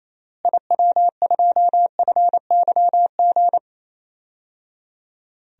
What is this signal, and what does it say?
Morse code, heard as a keyed tone, about 28 wpm, 700 Hz. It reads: IW2FYG
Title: DXCC-100 Call Signs - 6 Characters - Repeated 3x - 28wpm